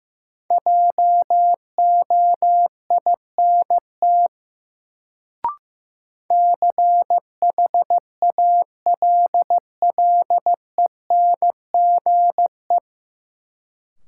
Morse code, heard as a keyed tone, about 16 wpm, 700 Hz. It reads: JOINT CHALLENGE
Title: Sentences from Top 1300 Words - New Word Drill Alt - 15wpm